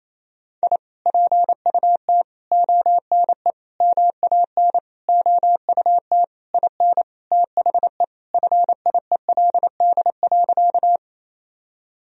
Morse code, heard as a keyed tone, about 28 wpm, 700 Hz. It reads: I PUT ONE MAN OUT IN THE FIELD.
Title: Sentences from Top 1000 Words - Review Part 2 - 28wpm